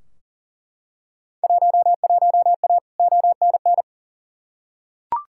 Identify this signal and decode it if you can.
Morse code, heard as a keyed tone, about 40 wpm, 700 Hz. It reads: 11A ONN
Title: Field Day Exchanges Encore - 40wpm